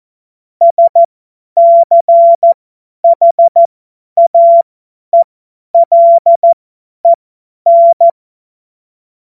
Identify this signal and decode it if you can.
Morse code, heard as a keyed tone, about 14 wpm, spaced as for 10 wpm, 680 Hz. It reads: SCHAELEN
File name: Morsekurs / Witze / Witz 13. April 2021, 50 F70